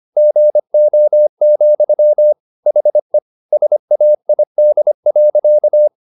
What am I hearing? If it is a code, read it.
Morse code, heard as a keyed tone, about 25 wpm, 600 Hz. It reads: GO, HE SAID.
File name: the old man and the sea part 1 25wpm